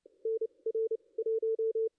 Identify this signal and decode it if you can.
Morse code, heard as a keyed tone, about 29 wpm, spaced as for 23 wpm, 440 Hz. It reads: NR 1
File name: contests / ss-cw-k5tr-op-n5rz -2300